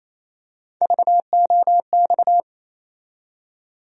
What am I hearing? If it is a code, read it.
Morse code, heard as a keyed tone, about 28 wpm, 700 Hz. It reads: VOX
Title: USA Call Sign Postfixes - 3 Letters 28wpm